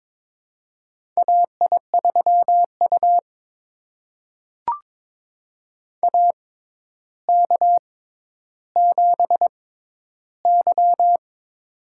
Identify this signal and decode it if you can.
Morse code, heard as a keyed tone, about 21 wpm, spaced as for 11 wpm, 700 Hz. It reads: AI3U AK7Y
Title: USA Call Signs - 4 Characters - 6x Character Spacing - 22wpm